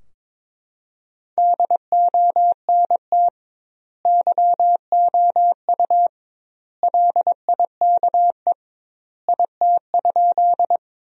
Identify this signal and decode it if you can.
Morse code, heard as a keyed tone, about 22 wpm, 700 Hz. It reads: DONT YOU LIKE IT?
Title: 2x Word-Spacing - Sentences from Top 300 Words - 22wpm